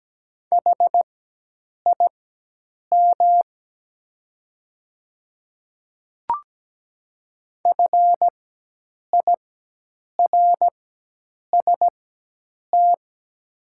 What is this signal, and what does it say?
Morse code, heard as a keyed tone, about 17 wpm, spaced as for 8 wpm, 700 Hz. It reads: HIM FIRST
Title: 4x Character Spacing - Top 100 Words - 17wpm